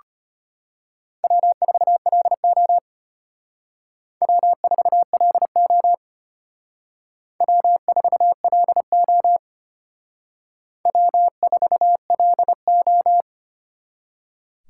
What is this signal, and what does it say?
Morse code, heard as a keyed tone, about 32 wpm, 700 Hz. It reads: W4LO W4LO W4LO W4LO